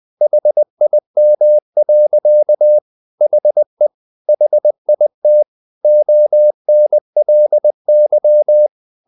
Morse code, heard as a keyed tone, about 20 wpm, 600 Hz. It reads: HIM. HE HIT ONLY
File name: the old man and the sea part 2 20wpm